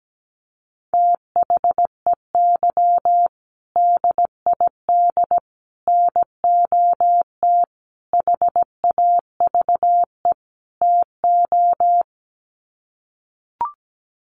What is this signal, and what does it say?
Morse code, heard as a keyed tone, about 17 wpm, 700 Hz. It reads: THEY DID NOT HAVE TO
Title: Sets of 5 Words Encore2 17wpm